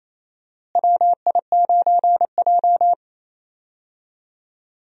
Morse code, heard as a keyed tone, about 28 wpm, 700 Hz. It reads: WI9J
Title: USA Call Signs - 4 Characters - Repeated 3x - 28wpm